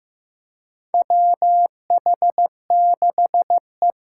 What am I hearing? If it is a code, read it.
Morse code, heard as a keyed tone, about 15 wpm, 700 Hz. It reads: WH6E
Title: USA Call Signs - 8x Character Spacing - 15wpm